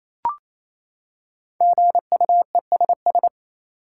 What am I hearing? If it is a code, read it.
Morse code, heard as a keyed tone, about 28 wpm, 700 Hz. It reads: GUESS
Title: Five Letter Words - 28wpm